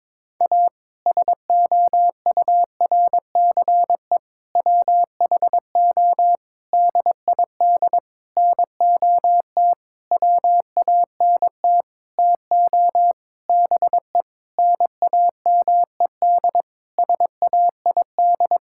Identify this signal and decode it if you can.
Morse code, heard as a keyed tone, about 22 wpm, 700 Hz. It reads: A SOURCE WHO DID NOT WANT TO BE NAMED SAID
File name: practice / 22wpm / Sentences with 8-Word Phrases - 22wpm